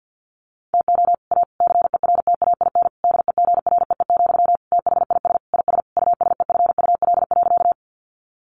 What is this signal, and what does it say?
Morse code, heard as a keyed tone, about 48 wpm, 700 Hz. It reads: TO A CERTAIN DEGREE, THIS IS USEFUL.
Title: Sentences from Top 1500 Words 50wpm